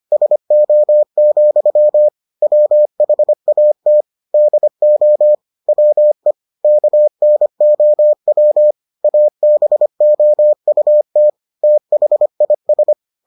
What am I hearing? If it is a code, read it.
Morse code, heard as a keyed tone, about 25 wpm, 600 Hz. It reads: SO, WHAT DO WE KNOW ABOUT THIS